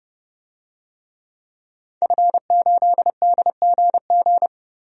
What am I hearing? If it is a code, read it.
Morse code, heard as a keyed tone, about 30 wpm, 700 Hz. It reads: F8DGG
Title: DXCC-100 Call Signs - 5 Characters - Repeated 3x - 30wpm